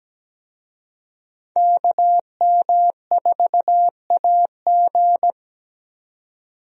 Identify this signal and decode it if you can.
Morse code, heard as a keyed tone, about 17 wpm, 700 Hz. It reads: KM4AG
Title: USA Call Signs - 5 Characters - 17wpm